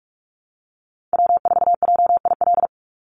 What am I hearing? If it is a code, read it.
Morse code, heard as a keyed tone, about 46 wpm, 700 Hz. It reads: W4JIL
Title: USA Call Signs 45wpm